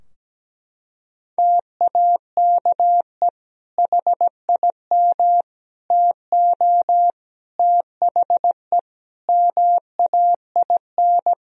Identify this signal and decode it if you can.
Morse code, heard as a keyed tone, about 17 wpm, 700 Hz. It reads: TAKE HIM TO THE MAIN
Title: Sentences from Top 1000 Words - Review Part 4 - 17wpm